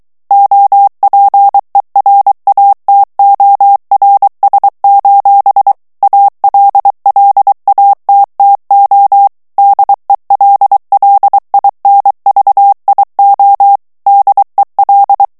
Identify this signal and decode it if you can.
Morse code, heard as a keyed tone, about 23 wpm, 800 Hz. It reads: OPERATORS: ALLATTO DELLINVIO DEL
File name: ITALIEN texte clair 2 vitesse 22 mots minute